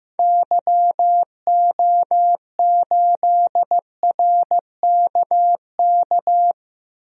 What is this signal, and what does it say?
Morse code, heard as a keyed tone, about 15 wpm, 700 Hz. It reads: YO8RKK
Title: DXCC-100 Call Signs - 6 Characters - 15wpm